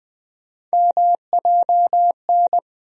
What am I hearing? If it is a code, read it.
Morse code, heard as a keyed tone, about 20 wpm, 700 Hz. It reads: MJN
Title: USA Call Sign Postfixes - 3 Letters 20wpm